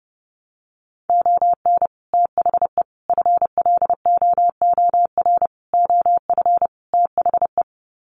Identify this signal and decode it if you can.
Morse code, heard as a keyed tone, about 30 wpm, 700 Hz. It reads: ON THE FLOOR OF THE
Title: Sets of 5 Words 30wpm